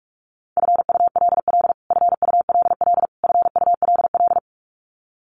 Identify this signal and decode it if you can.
Morse code, heard as a keyed tone, about 44 wpm, 700 Hz. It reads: FULL FULL FULL
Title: Top 500 Words Repeated 45wpm